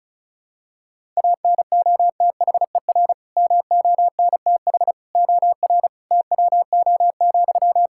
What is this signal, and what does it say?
Morse code, heard as a keyed tone, about 34 wpm, 700 Hz. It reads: ANOTHER MONTH OR TWO,
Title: Sentences from Top 1000 Words - Review Part 5 - 35wpm